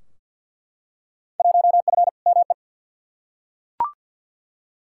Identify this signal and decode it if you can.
Morse code, heard as a keyed tone, about 49 wpm, 700 Hz. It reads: 1F ME
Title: Field Day Exchanges Encore - 50wpm